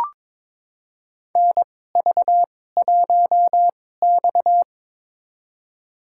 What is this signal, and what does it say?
Morse code, heard as a keyed tone, about 22 wpm, spaced as for 16 wpm, 700 Hz. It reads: NV1X
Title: USA Call Signs - 4 Characters - 2x Character Spacing - 22wpm